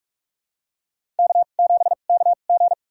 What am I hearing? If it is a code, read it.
Morse code, heard as a keyed tone, about 42 wpm, spaced as for 31 wpm, 700 Hz. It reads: K7KG
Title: USA Call Signs - 4 Characters - 2x Character Spacing - 45wpm